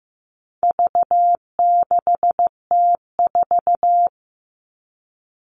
Morse code, heard as a keyed tone, about 15 wpm, 700 Hz. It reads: V6T4
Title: All Cut Numbers - 4-Digit 15wpm